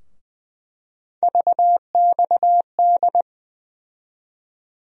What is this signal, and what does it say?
Morse code, heard as a keyed tone, about 20 wpm, 700 Hz. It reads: VXD
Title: USA Call Sign Postfixes - 3 Letters 20wpm